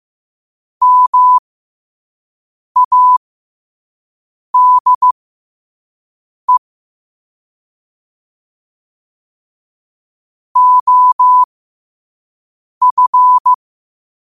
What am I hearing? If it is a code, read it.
Morse code, heard as a keyed tone, about 15 wpm, spaced as for 5 wpm, 1000 Hz. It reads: MADE OF
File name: qotdfiles / 5wpm